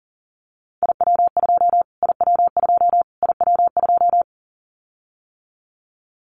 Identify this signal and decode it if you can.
Morse code, heard as a keyed tone, about 40 wpm, 700 Hz. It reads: IW2 IW2 IW2
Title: Callsign Prefixes for Top 25 DXCC - Repeated 3x - 40wpm